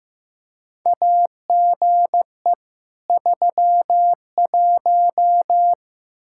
Morse code, heard as a keyed tone, about 15 wpm, 700 Hz. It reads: AGE 31
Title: Practice QSO - Exchange 3 - More Info - 15wpm